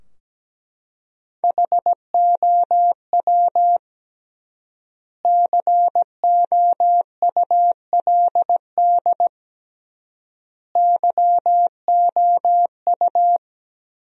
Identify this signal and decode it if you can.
Morse code, heard as a keyed tone, about 17 wpm, 700 Hz. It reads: HOW COULD YOU
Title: 3x Word-Spacing - Sentences from Top 300 Words - 17wpm